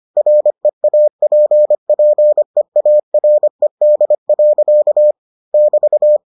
Morse code, heard as a keyed tone, about 25 wpm, 600 Hz. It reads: REAPPEARED. <BT>